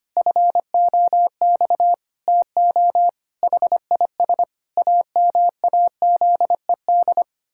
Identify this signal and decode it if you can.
Morse code, heard as a keyed tone, about 25 wpm, 700 Hz. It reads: FOX TO HIS AMAZED
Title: Pangram Challenge v2 - 25wpm